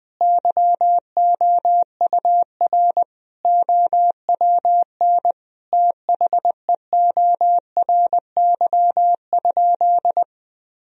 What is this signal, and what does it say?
Morse code, heard as a keyed tone, about 20 wpm, 700 Hz. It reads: YOUR OWN THEORY?
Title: Sentences from Top 1000 Words - Review Part 3 - 20wpm